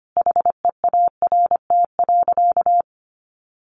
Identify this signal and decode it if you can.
Morse code, heard as a keyed tone, about 25 wpm, 700 Hz. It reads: HEART.
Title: Sentences from Top 1400 Words 25wpm